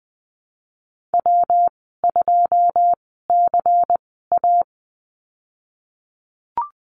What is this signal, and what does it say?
Morse code, heard as a keyed tone, about 19 wpm, spaced as for 14 wpm, 700 Hz. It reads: W2CA
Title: 167 Lesson 27g - Number 7 - Call Signs 20wpm